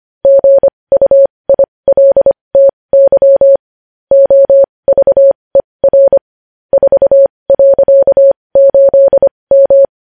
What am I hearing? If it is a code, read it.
Morse code, heard as a keyed tone, about 25 wpm, spaced as for 20 wpm, 550 Hz. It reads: GUILTY OVER 4.8M
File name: podcast / News headlines-1x-25x20